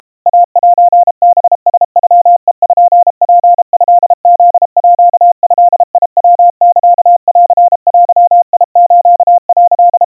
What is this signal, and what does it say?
Morse code (Japanese Wabun code), heard as a keyed tone, about 32 wpm, 700 Hz. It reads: イセハラノペツトフードヤサンデス」
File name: kawa / Morse sample / kamyu 120 cpm 700 Hz Crispy